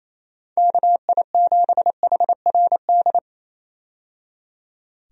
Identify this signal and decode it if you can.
Morse code, heard as a keyed tone, about 28 wpm, 700 Hz. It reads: KI7HRD
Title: USA Call Signs - 6 Characters - 28wpm